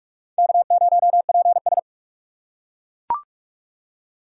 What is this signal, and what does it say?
Morse code, heard as a keyed tone, about 40 wpm, 700 Hz. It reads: K0PS